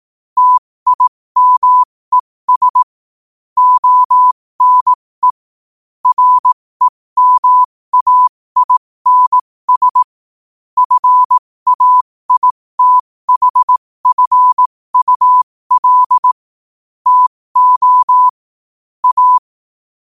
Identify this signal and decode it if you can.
Morse code, heard as a keyed tone, about 18 wpm, spaced as for 15 wpm, 1000 Hz. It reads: TIMES ONE REMAINS FAITHFUL TO A
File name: 15wpm